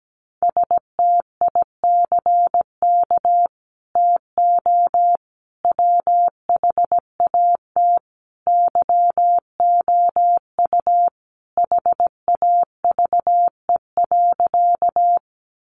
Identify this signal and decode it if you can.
Morse code, heard as a keyed tone, about 17 wpm, 700 Hz. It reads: STICK TO WHAT YOU HAVE.
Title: Sentences from Top 1200 Words 17wpm